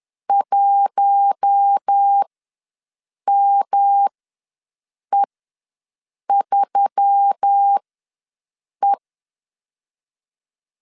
Morse code, heard as a keyed tone, about 10 wpm, spaced as for 6 wpm, 800 Hz. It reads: 1ME3E
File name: LEZIONI / 5 LBVC / 5 L ws 10wpm 3636